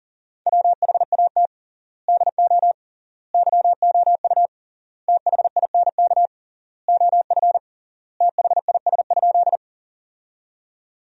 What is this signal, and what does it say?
Morse code, heard as a keyed tone, about 40 wpm, 700 Hz. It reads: WHAT DO YOU THINK OF THIS?
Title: 3x Word-Spacing - Sentences from Top 200 Words - 40wpm